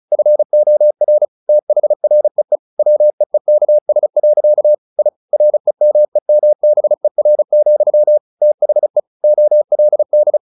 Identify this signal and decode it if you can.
Morse code, heard as a keyed tone, about 34 wpm, 600 Hz. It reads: FOR THREE WEEKS. I REMEMBER, THE OLD
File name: the old man and the sea / the old man and the sea 1 35wpm